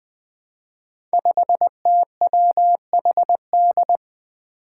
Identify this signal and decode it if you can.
Morse code, heard as a keyed tone, about 20 wpm, 700 Hz. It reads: 5TWHD
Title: Five Challenging Letter-Number Combinations 20wpm